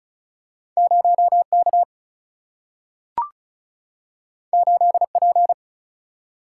Morse code, heard as a keyed tone, about 36 wpm, 700 Hz. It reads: 0 K 8 P
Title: Single-Letter Single-Number 35wpm